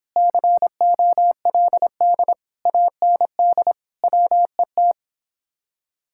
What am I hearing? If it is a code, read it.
Morse code, heard as a keyed tone, about 29 wpm, 700 Hz. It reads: COLD AND WET